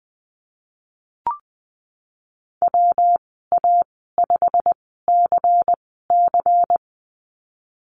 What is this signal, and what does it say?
Morse code, heard as a keyed tone, about 20 wpm, spaced as for 15 wpm, 700 Hz. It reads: WA5CC
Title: 161 Lesson 26g - Letter g - Call Signs 20wpm